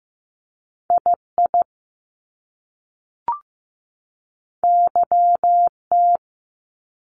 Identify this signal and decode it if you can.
Morse code, heard as a keyed tone, about 15 wpm, 700 Hz. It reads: II YT
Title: Top Two Letter Combinations in Words 15wpm